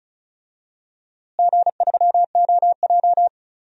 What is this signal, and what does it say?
Morse code, heard as a keyed tone, about 34 wpm, 700 Hz. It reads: G3OJ
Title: DXCC-100 Call Signs - 4 Characters - Repeated 2x - 35wpm